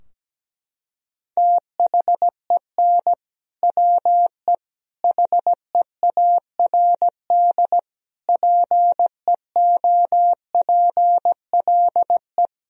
Morse code, heard as a keyed tone, about 17 wpm, 700 Hz. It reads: THEN WE HEARD PEOPLE
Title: Sentences from Top 1000 Words - Review Part 2 - 17wpm